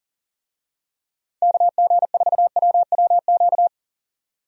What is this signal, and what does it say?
Morse code, heard as a keyed tone, about 40 wpm, 700 Hz. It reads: KG4WWQ
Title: USA Call Signs - 6 Characters - 40wpm